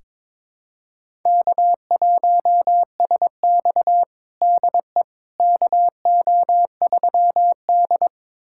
Morse code, heard as a keyed tone, about 22 wpm, 700 Hz. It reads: K1SX DE KO3D
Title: USA Callsign DE Callsign - 4 Characters - 22wpm